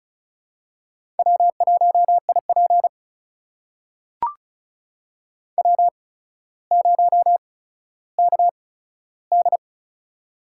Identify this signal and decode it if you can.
Morse code, heard as a keyed tone, about 36 wpm, spaced as for 15 wpm, 700 Hz. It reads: W1IP W0KD
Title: USA Call Signs - 4 Characters - 8x Character Spacing - 35wpm